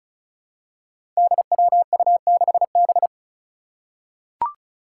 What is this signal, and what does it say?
Morse code, heard as a keyed tone, about 36 wpm, 700 Hz. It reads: DWU6B
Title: Five Challenging Letter-Number Combinations 35wpm